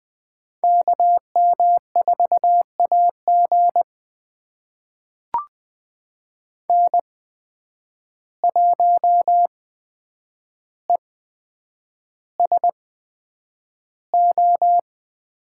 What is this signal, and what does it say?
Morse code, heard as a keyed tone, about 20 wpm, spaced as for 9 wpm, 700 Hz. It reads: KM4AG N1ESO
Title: USA Call Signs - 8x Character Spacing - 20wpm